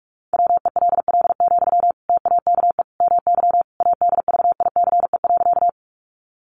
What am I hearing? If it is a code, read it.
Morse code, heard as a keyed tone, about 44 wpm, 700 Hz. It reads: WELL, TAKE MY ADVICE.
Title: Sentences from Top 1100 Words 45wpm